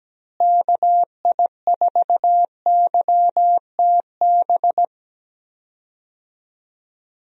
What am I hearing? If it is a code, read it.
Morse code, heard as a keyed tone, about 17 wpm, 700 Hz. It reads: KI4YTB
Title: USA Call Signs - 6 Characters - Repeated 2x - 17wpm